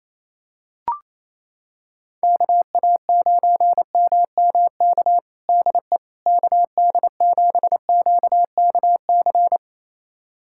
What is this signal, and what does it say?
Morse code, heard as a keyed tone, about 28 wpm, 700 Hz. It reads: KA9MMK DE KD7QKC
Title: USA Callsign DE Callsign 28wpm